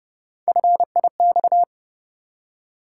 Morse code, heard as a keyed tone, about 30 wpm, 700 Hz. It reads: FIX
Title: Three Letter Words - 30wpm